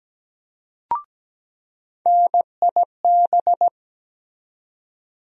Morse code, heard as a keyed tone, about 17 wpm, 700 Hz. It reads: NIB
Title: USA Call Sign Postfixes - 3 Letters 17wpm